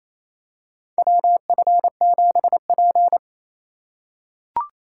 Morse code, heard as a keyed tone, about 28 wpm, 700 Hz. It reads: WF7P
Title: USA Call Signs - 2x Character Spacing - 28wpm